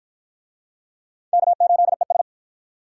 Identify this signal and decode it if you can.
Morse code, heard as a keyed tone, about 53 wpm, 700 Hz. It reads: K8ES